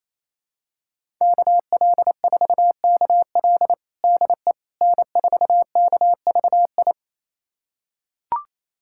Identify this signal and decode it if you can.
Morse code, heard as a keyed tone, about 28 wpm, 700 Hz. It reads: KL4KL DE N4KVI
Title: USA Callsign DE Callsign - 5 Characters - 28wpm